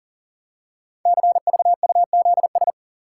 Morse code, heard as a keyed tone, about 42 wpm, 700 Hz. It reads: CVUZS
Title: Code Groups - 5 Letters - 40wpm